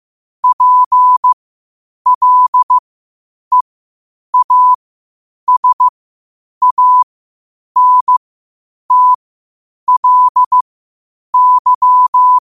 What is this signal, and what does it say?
Morse code, heard as a keyed tone, about 15 wpm, spaced as for 8 wpm, 1000 Hz. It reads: PLEASANTLY
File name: qotdfiles / 8wpm